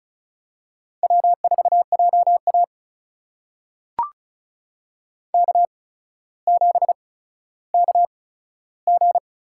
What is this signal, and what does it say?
Morse code, heard as a keyed tone, about 36 wpm, spaced as for 15 wpm, 700 Hz. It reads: W4JA K7KG
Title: USA Call Signs - 4 Characters - 8x Character Spacing - 35wpm